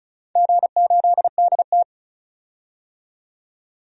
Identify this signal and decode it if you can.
Morse code, heard as a keyed tone, about 34 wpm, 700 Hz. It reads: G8DT
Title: DXCC-100 Call Signs - 4 Characters - 35wpm